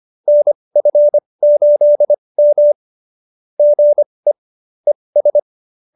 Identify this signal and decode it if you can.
Morse code, heard as a keyed tone, about 25 wpm, spaced as for 20 wpm, 590 Hz. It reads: NF8M GE ES
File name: QSO321 20